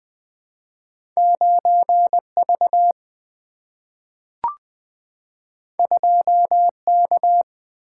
Morse code, heard as a keyed tone, about 20 wpm, 700 Hz. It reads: 9V 2K
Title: Single-Letter Single-Number 20wpm